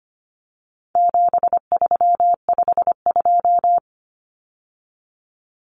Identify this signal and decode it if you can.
Morse code, heard as a keyed tone, about 25 wpm, 700 Hz. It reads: 7352
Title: Numbers - 4-Digit 25wpm